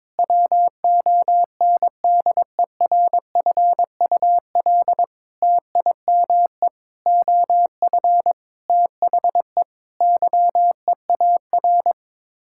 Morse code, heard as a keyed tone, about 22 wpm, 700 Hz. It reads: WONDERFUL TIME OF THE YEAR
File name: practice / 22wpm / Sets of 8 Words - 22wpm